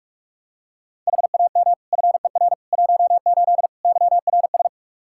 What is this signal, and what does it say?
Morse code, heard as a keyed tone, about 46 wpm, 700 Hz. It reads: HAM FER 18 YRS
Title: Practice QSO - Exchange 3 - More Info - 45wpm